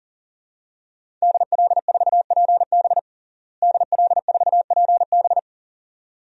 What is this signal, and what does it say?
Morse code, heard as a keyed tone, about 37 wpm, 700 Hz. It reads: DL4PB DL4PB
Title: DXCC-100 Call Signs - 5 Characters - Repeated 2x - 40wpm